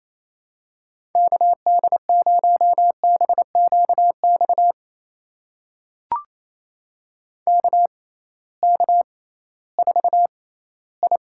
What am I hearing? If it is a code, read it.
Morse code, heard as a keyed tone, about 28 wpm, spaced as for 16 wpm, 700 Hz. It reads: KD0BQX KK4I
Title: USA Call Signs - 6 Characters - 6x Character Spacing - 28wpm